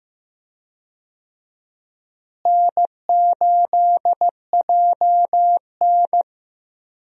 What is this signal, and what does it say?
Morse code, heard as a keyed tone, about 15 wpm, 700 Hz. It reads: N8JN
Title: USA Call Signs - 4 Characters - Repeated 2x - 15wpm